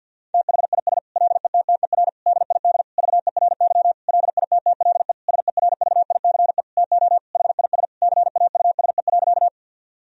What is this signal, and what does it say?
Morse code, heard as a keyed tone, about 52 wpm, 700 Hz. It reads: THIS LETTER DID VERY LITTLE SERVICE TO HIS CAUSE.
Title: Sentences from Top 1000 Words - Review Part 4 - 50wpm